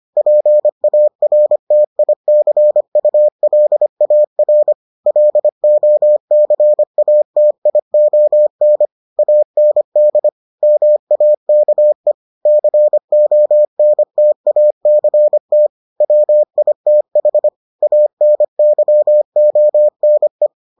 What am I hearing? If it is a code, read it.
Morse code, heard as a keyed tone, about 25 wpm, 600 Hz. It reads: PARTICULAR LOCATION AND MAKE CONTACT WITH ANYONE